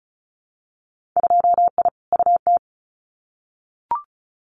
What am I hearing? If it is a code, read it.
Morse code, heard as a keyed tone, about 36 wpm, 700 Hz. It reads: 2I UT
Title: Winter Field Day Exchanges 35wpm